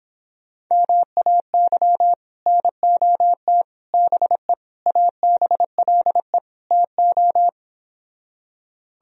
Morse code, heard as a keyed tone, about 31 wpm, 700 Hz. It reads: MAY NOT BE ABLE TO